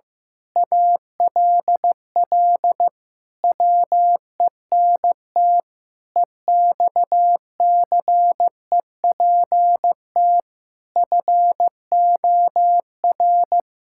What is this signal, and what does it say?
Morse code, heard as a keyed tone, about 15 wpm, 700 Hz. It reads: ALL WENT EXCEPT FOR
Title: Sentences from Top 1600 Words 15wpm